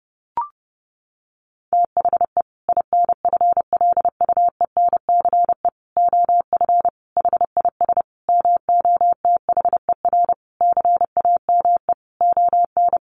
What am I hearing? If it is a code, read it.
Morse code, heard as a keyed tone, about 30 wpm, 700 Hz. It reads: THE INFLUENCE OF HIS MOTHER CAME ON
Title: Sentences from Top 1100 Words 30wpm